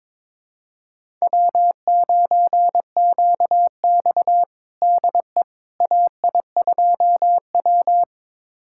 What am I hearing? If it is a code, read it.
Morse code, heard as a keyed tone, about 22 wpm, 700 Hz. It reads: W9QX DE AI2W
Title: USA Callsign DE Callsign - 4 Characters - 22wpm